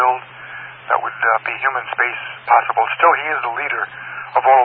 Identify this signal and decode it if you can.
Emergency Alert System (SAME) tones, heard as harsh buzzing bursts, with speech. It is heard from 0.3 to 4.2 s.